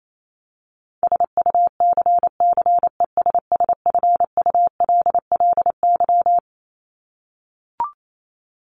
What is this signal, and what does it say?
Morse code, heard as a keyed tone, about 28 wpm, 700 Hz. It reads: SUCCESSFULLY
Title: Top 1501-1750 Words 28wpm